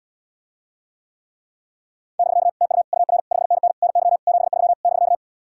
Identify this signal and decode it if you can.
Morse code, heard as a keyed tone, about 80 wpm, 700 Hz. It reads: <BK> TU UR 5NN NY 73 <BK>